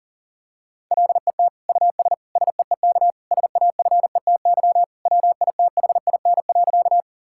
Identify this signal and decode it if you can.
Morse code, heard as a keyed tone, about 40 wpm, 700 Hz. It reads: LET US SEEK SAFETY WITHIN.
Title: Sentences from Top 1600 Words 40wpm